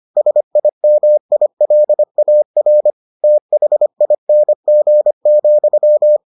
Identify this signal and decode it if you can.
Morse code, heard as a keyed tone, about 25 wpm, 600 Hz. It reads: SIMILAR THING,